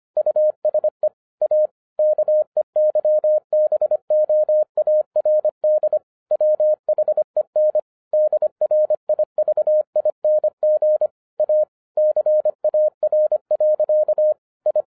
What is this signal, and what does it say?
Morse code, heard as a keyed tone, about 25 wpm, 610 Hz. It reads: USE A KEYBOARD WHEN DRIVING A CAR. I
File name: 2011-10-mp3 / n09-CW TO PS2 CONVERTER 25wpm